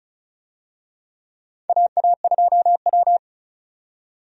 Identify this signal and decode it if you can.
Morse code, heard as a keyed tone, about 36 wpm, 700 Hz. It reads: AA2W
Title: USA Call Signs - 4 Characters - 35wpm